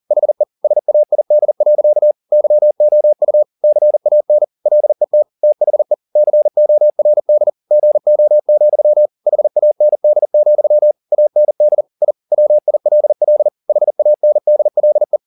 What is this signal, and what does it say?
Morse code, heard as a keyed tone, about 40 wpm, 600 Hz. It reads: HE SAID. YOU CAN LET THE CORD GO, HAND, AND I WILL HANDLE